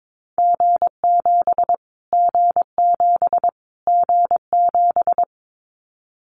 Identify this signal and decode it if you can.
Morse code, heard as a keyed tone, about 22 wpm, 700 Hz. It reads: G7 G7 G7
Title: Callsign Prefixes for Top 25 DXCC - Repeated 3x - 22wpm